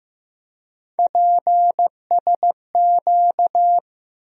Morse code, heard as a keyed tone, about 15 wpm, 700 Hz. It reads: PSQ
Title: USA Call Sign Postfixes - 3 Letters 15wpm